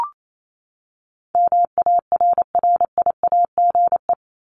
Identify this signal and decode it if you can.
Morse code, heard as a keyed tone, about 28 wpm, 700 Hz. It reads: MARRIAGE
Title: Top 751-1000 Words 28wpm